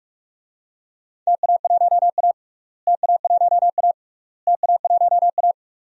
Morse code, heard as a keyed tone, about 46 wpm, 700 Hz. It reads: TA1A TA1A TA1A
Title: DXCC-100 Call Signs - 4 Characters - Repeated 3x - 45wpm